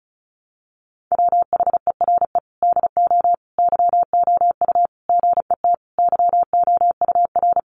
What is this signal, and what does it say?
Morse code, heard as a keyed tone, about 34 wpm, 700 Hz. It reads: WHERE DO YOU GET YOUR
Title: Sentences from Top 600 Words 35wpm